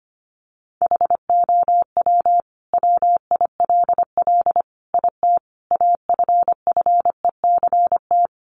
Morse code, heard as a keyed tone, about 25 wpm, 700 Hz. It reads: HOW WILL IT AFFECT
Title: Sentences from Top 900 Words 25wpm